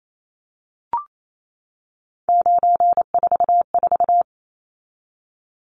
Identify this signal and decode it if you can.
Morse code, heard as a keyed tone, about 28 wpm, 700 Hz. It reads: 944
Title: Numbers - 3-Digit 28wpm